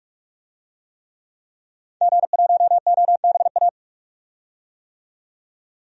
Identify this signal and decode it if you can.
Morse code, heard as a keyed tone, about 47 wpm, 700 Hz. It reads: G1OBA
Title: DXCC-100 Call Signs - 5 Characters - 45wpm